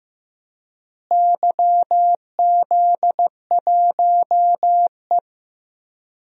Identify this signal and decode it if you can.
Morse code, heard as a keyed tone, about 16 wpm, 700 Hz. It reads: YZ1E
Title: DXCC-100 Call Signs - 4 Characters - Repeated 2x - 15wpm